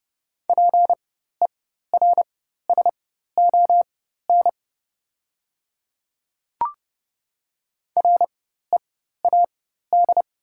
Morse code, heard as a keyed tone, about 30 wpm, spaced as for 14 wpm, 690 Hz. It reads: PERSON READ
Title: 4x Character Spacing - Top 500 Words - 30wpm